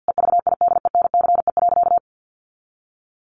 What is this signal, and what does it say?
Morse code, heard as a keyed tone, about 50 wpm, 700 Hz. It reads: EVIDENCE.
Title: Sentences from Top 1300 Words 50wpm